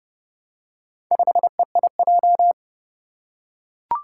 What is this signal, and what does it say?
Morse code, heard as a keyed tone, about 30 wpm, 700 Hz. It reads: 5EIJ
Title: Four Challenging Letter-Number Combinations 30wpm